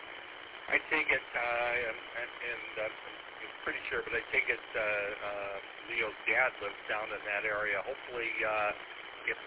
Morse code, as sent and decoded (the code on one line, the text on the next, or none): none